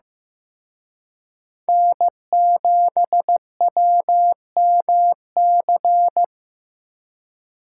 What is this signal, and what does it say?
Morse code, heard as a keyed tone, about 18 wpm, 700 Hz. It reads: N7WMC
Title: Speed Racing - USA 1x3 Call Signs 15wpm